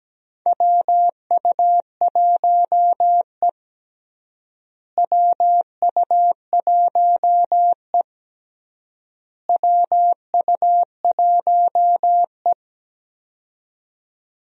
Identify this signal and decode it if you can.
Morse code, heard as a keyed tone, about 17 wpm, 700 Hz. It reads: WU1E WU1E WU1E
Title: USA Call Signs - 4 Characters - Repeated 3x - 17wpm